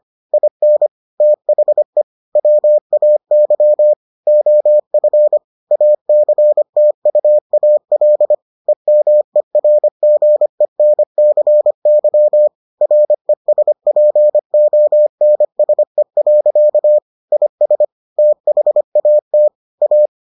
Morse code, heard as a keyed tone, about 25 wpm, 600 Hz. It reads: IN THE WAY OF ACTUAL EMERGENCY RESPONSE. IS THAT A